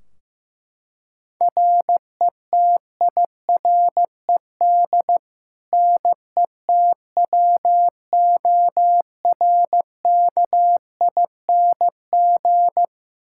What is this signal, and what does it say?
Morse code, heard as a keyed tone, about 15 wpm, 700 Hz. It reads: RETIRED NETWORKING
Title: K1USN SST QSO Practice 15wpm